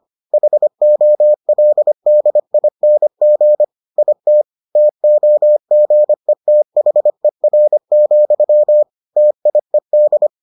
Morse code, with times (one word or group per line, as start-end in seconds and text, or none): 0.3-3.6 HOLDING
4.0-4.4 IT
4.8-8.8 TOGETHER,
9.2-10.3 TIED